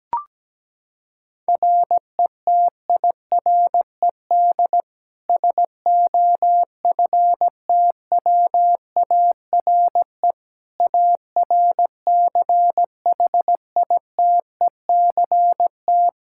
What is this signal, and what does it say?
Morse code, heard as a keyed tone, about 17 wpm, 700 Hz. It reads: RETIRED SOFTWARE ARCHITECT
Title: Practice QSO - Exchange 3 - More Info - 17wpm